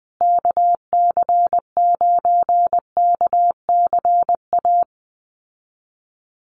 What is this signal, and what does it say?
Morse code, heard as a keyed tone, about 19 wpm, 700 Hz. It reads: KC9KCA
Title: USA Call Signs Repeated 3x 20wpm